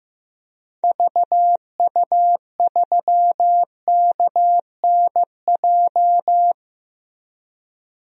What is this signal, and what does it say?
Morse code, heard as a keyed tone, about 15 wpm, 700 Hz. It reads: VU3KNJ
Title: DXCC-100 Call Signs - 6 Characters - Repeated 2x - 15wpm